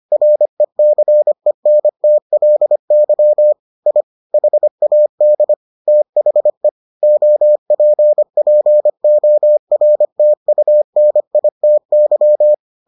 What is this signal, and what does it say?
Morse code, heard as a keyed tone, about 25 wpm, 600 Hz. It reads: RECENTLY I HAD THE OPPORTUNITY